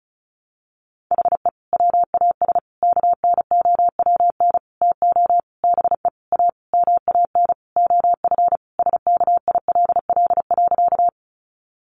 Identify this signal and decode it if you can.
Morse code, heard as a keyed tone, about 34 wpm, 700 Hz. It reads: HE WAS KNOWN TO BE A MAN OF SKILL.
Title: Sentences from Top 1200 Words 35wpm